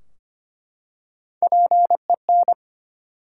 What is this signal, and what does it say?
Morse code, heard as a keyed tone, about 25 wpm, 700 Hz. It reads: PEN
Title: Three Letter Words - 25wpm